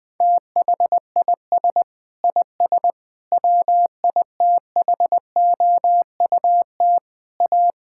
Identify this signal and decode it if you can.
Morse code, heard as a keyed tone, about 20 wpm, 700 Hz. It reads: THIS IS WITHOUT A
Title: Sentences with 7-Word Phrases - 20wpm